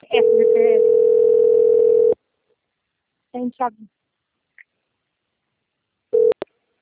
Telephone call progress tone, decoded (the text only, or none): ringback tone